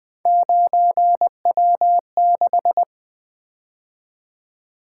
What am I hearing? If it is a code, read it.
Morse code, heard as a keyed tone, about 21 wpm, 700 Hz. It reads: 9W6
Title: Three Letter-Number Combinations - 20wpm